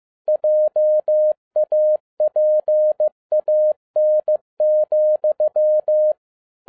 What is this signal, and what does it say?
Morse code, heard as a keyed tone, about 15 wpm, 610 Hz. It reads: JAPAN,
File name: 2011-10-mp3 / n02-FIELD DAY CONTEST QRV FROM SAKAE VILLAGE 15wpm